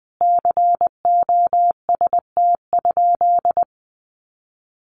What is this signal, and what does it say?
Morse code, heard as a keyed tone, about 20 wpm, 700 Hz. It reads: COST?
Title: Sentences from Top 800 Words 20wpm